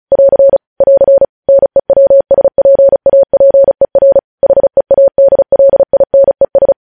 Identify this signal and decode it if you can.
Morse code, heard as a keyed tone, about 36 wpm, 550 Hz. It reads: <AR> <AR> NEWSPAPER HEADLINES
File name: podcast / News headlines-1x-35x35